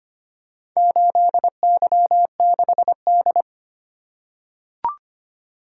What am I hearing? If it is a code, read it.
Morse code, heard as a keyed tone, about 25 wpm, 700 Hz. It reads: 8Y6D
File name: practice / 25wpm / Four Letter-Number Combinations - 25wpm